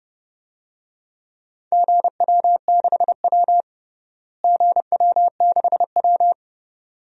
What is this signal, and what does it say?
Morse code, heard as a keyed tone, about 30 wpm, 700 Hz. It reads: GW6W GW6W
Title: DXCC-100 Call Signs - 4 Characters - Repeated 2x - 30wpm